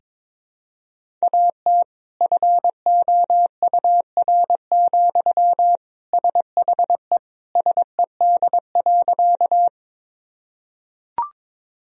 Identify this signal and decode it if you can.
Morse code, heard as a keyed tone, about 22 wpm, 700 Hz. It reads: AT FOUR, SHE SED.
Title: Sentences from Top 300 Words and Abbreviations 22wpm